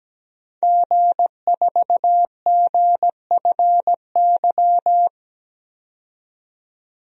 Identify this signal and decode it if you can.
Morse code, heard as a keyed tone, about 17 wpm, 700 Hz. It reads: G4GFY
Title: DXCC-100 Call Signs - 5 Characters - 17wpm